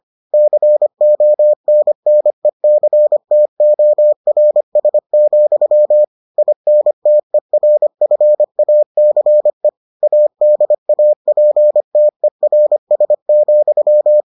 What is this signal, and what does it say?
Morse code, heard as a keyed tone, about 25 wpm, 600 Hz. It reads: CONNECTORS, INTERFACE ADAPTERS,